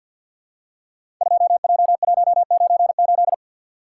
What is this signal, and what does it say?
Morse code, heard as a keyed tone, about 50 wpm, 700 Hz. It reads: 2J198
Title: Code Groups - 5 Letter:Numbers - 50wpm